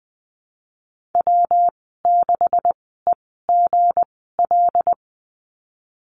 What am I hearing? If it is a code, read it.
Morse code, heard as a keyed tone, about 20 wpm, spaced as for 15 wpm, 700 Hz. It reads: W6EGL
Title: 167 Lesson 27g - Number 7 - Call Signs 20wpm